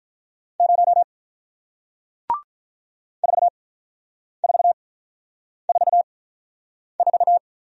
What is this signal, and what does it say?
Morse code, heard as a keyed tone, about 48 wpm, spaced as for 9 wpm, 700 Hz. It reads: 0 4444